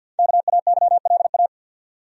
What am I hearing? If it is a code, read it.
Morse code, heard as a keyed tone, about 47 wpm, 700 Hz. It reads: KAYLA
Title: USA Common Names - Max 5 Letters - 50wpm